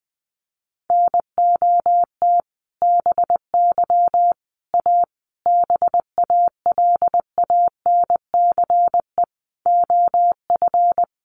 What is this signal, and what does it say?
Morse code, heard as a keyed tone, about 20 wpm, 700 Hz. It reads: NOT BY A BALANCE OF
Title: Sentences from Top 900 Words 20wpm